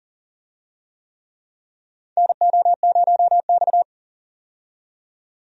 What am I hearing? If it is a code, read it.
Morse code, heard as a keyed tone, about 40 wpm, 700 Hz. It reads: NO0X
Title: USA Call Signs - 4 Characters - 40wpm